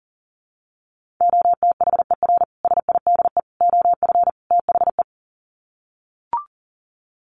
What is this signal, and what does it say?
Morse code, heard as a keyed tone, about 40 wpm, 700 Hz. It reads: OTHER SIDE OF THE
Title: Sets of 4 Words 40wpm